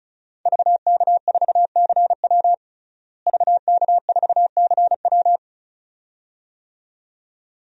Morse code, heard as a keyed tone, about 34 wpm, 700 Hz. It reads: VK4CW VK4CW
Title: DXCC-100 Call Signs - 5 Characters - Repeated 2x - 35wpm